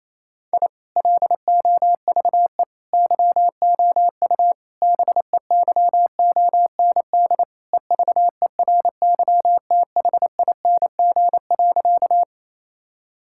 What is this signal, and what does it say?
Morse code, heard as a keyed tone, about 28 wpm, 700 Hz. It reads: I LOVE YOU BEYOND EVERYTHING.
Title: Sentences from Top 1000 Words - Review Part 6 - 28wpm